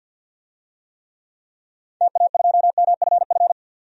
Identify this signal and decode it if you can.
Morse code, heard as a keyed tone, about 52 wpm, 700 Hz. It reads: TA2MFF
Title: DXCC-100 Call Signs - 6 Characters - 50wpm